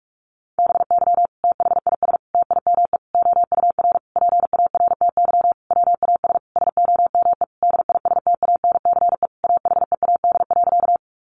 Morse code, heard as a keyed tone, about 44 wpm, 700 Hz. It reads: BY THIS TIME OUR PARTY WAS SOME DISTANCE AHEAD.
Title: Sentences from Top 1400 Words 45wpm